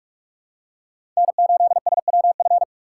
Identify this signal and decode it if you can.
Morse code, heard as a keyed tone, about 46 wpm, 700 Hz. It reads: N8SWF
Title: USA Call Signs - 2x Character Spacing - 45wpm